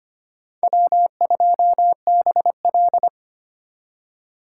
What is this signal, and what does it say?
Morse code, heard as a keyed tone, about 32 wpm, 700 Hz. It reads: W2BL